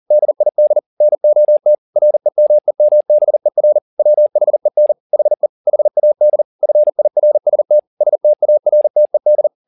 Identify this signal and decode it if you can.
Morse code, heard as a keyed tone, about 40 wpm, 600 Hz. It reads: DID NOT REMEMBER WHEN HE HAD FIRST STARTED